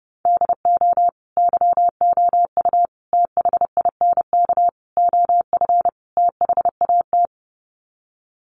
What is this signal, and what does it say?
Morse code, heard as a keyed tone, about 30 wpm, 700 Hz. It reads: DO YOU THINK OF THAT
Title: Sets of 5 Words Encore2 30wpm